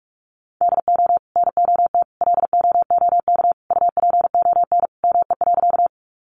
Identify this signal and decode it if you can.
Morse code, heard as a keyed tone, about 44 wpm, 700 Hz. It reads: DO NOT LOOK UPON ME.
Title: Sentences from Top 1100 Words 45wpm